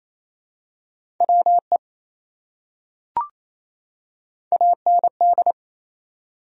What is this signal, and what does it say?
Morse code, heard as a keyed tone, about 28 wpm, 700 Hz. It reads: WE AND
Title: Top 100 Words 28wpm